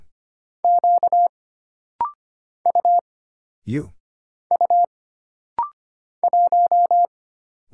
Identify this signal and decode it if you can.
Morse code, heard as a keyed tone, about 25 wpm, 700 Hz. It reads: Q U U 1
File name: Single Character (Letter:Number:Punctuation:Prosign) - Rapid-Fire - 25wpm